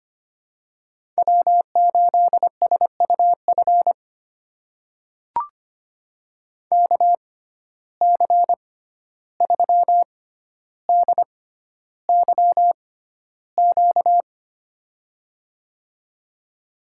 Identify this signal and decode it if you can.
Morse code, heard as a keyed tone, about 25 wpm, spaced as for 12 wpm, 700 Hz. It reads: W8SUF KC3DYQ
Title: USA Call Signs - 6x Character Spacing - 25wpm